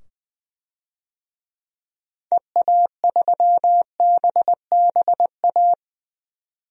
Morse code, heard as a keyed tone, about 21 wpm, 700 Hz. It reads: EA3BBA
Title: DXCC-100 Call Signs - 6 Characters - 20wpm